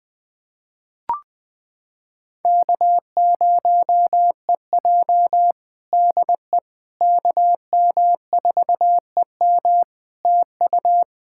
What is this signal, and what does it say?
Morse code, heard as a keyed tone, about 20 wpm, 700 Hz. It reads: K0EJ DE KM4EM TU
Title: Practice QSO - Exchange 4 - Ending - 20wpm